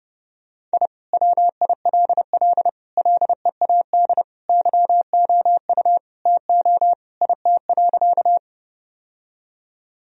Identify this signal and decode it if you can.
Morse code, heard as a keyed tone, about 32 wpm, 700 Hz. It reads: I WILL LEAD YOU TO IT.
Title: Sentences from Top 1000 Words - Review Part 2 - 30wpm